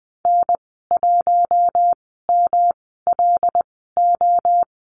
Morse code, heard as a keyed tone, about 19 wpm, spaced as for 14 wpm, 700 Hz. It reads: N1MLO
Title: 147 Lesson 24g - Letter y - Call Signs 20wpm